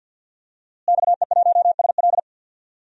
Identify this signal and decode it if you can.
Morse code, heard as a keyed tone, about 50 wpm, 700 Hz. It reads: XE1SL
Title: DXCC-100 Call Signs - 5 Characters - Repeated 3x - 50wpm